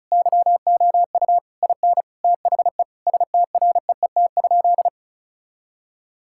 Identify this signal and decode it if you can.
Morse code, heard as a keyed tone, about 34 wpm, 700 Hz. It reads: YOU IN THE STREET?
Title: Sentences from Top 1000 Words - Review Part 6 - 35wpm